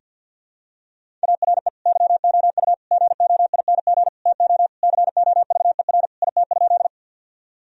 Morse code, heard as a keyed tone, about 52 wpm, 700 Hz. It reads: ARE YOU GOING TO COVER IT?
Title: Sentences from Top 1000 Words - Review Part 1 - 50wpm